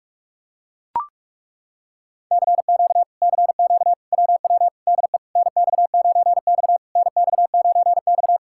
Morse code, heard as a keyed tone, about 38 wpm, 700 Hz. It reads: CQ CQ WW DE NX9X NX9X
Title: Speed-Racing Common QSO Phrases - 30wpm